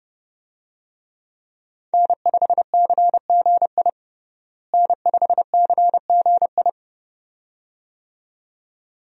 Morse code, heard as a keyed tone, about 30 wpm, 700 Hz. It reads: N5CGI N5CGI
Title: USA Call Signs - 5 Characters - Repeated 2x - 30wpm